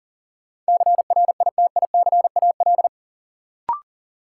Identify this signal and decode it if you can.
Morse code, heard as a keyed tone, about 38 wpm, 700 Hz. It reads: CRITICAL
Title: Sentences from Top 1100 Words - New Word Drill - 40wpm